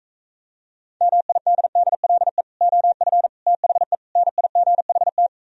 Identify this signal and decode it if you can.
Morse code, heard as a keyed tone, about 45 wpm, 690 Hz. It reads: MIDDLE OF THE NIGHT